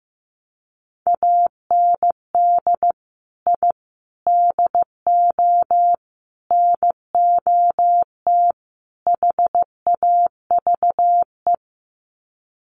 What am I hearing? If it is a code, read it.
Morse code, heard as a keyed tone, about 15 wpm, 700 Hz. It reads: AND I DO NOT HAVE
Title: Sets of 5 Words Encore 15wpm